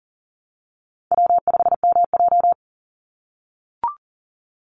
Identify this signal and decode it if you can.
Morse code, heard as a keyed tone, about 40 wpm, 700 Hz. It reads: W5MJ
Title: Easy Call Signs - 40wpm